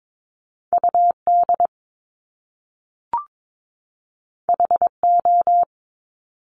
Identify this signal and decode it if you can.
Morse code, heard as a keyed tone, about 22 wpm, 700 Hz. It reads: UD HO
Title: Top Two Letter Combinations in Words 22wpm